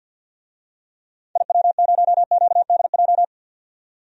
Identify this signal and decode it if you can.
Morse code, heard as a keyed tone, about 51 wpm, 700 Hz. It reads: IW0QDJ
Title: DXCC-100 Call Signs - 6 Characters - 50wpm